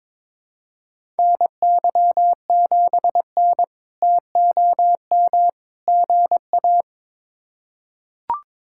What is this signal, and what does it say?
Morse code, heard as a keyed tone, about 22 wpm, 700 Hz. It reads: NY7N TOM GA
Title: K1USN SST QSO Practice - 22wpm